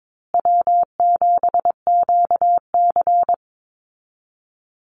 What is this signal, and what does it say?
Morse code, heard as a keyed tone, about 22 wpm, 700 Hz. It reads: W7QC
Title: Easy Call Signs Repeated 3x - 22wpm